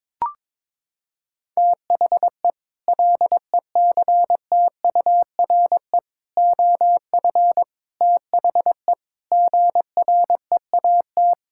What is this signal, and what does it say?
Morse code, heard as a keyed tone, about 22 wpm, 700 Hz. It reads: THE LECTURE OF THE GREAT
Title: Sentences from Top 1600 Words 22wpm